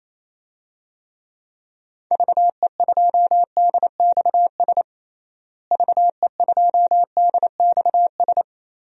Morse code, heard as a keyed tone, about 28 wpm, 700 Hz. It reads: VE2DXS VE2DXS
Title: DXCC-100 Call Signs - 6 Characters - Repeated 3x - 28wpm